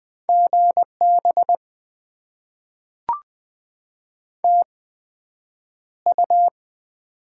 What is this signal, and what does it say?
Morse code, heard as a keyed tone, about 20 wpm, spaced as for 9 wpm, 700 Hz. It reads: GB TU
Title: Top 90 QSO Elements - 8x Character Spacing - 20wpm